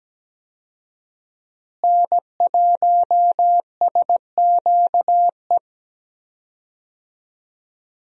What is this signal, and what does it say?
Morse code, heard as a keyed tone, about 17 wpm, 700 Hz. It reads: N1SQE
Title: USA Call Signs - 5 Characters - Repeated 2x - 17wpm